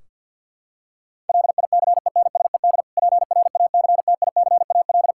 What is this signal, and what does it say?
Morse code, heard as a keyed tone, about 52 wpm, 700 Hz. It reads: LICENSED PRACTICAL
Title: Practice QSO - Exchange 3 - More Info - 50wpm